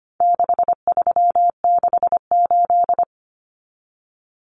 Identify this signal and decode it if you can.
Morse code, heard as a keyed tone, about 25 wpm, 700 Hz. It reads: 6368
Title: Common Cut Numbers - 4-Digit 25wpm